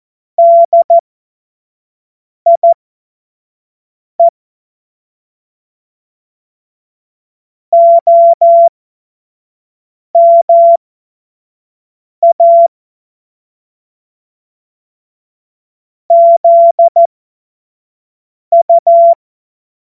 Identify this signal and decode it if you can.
Morse code, heard as a keyed tone, about 14 wpm, spaced as for 5 wpm, 680 Hz. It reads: DIE OMA ZU
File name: Morsekurs / Witze / Witz 20. Dezember 2022 25 F70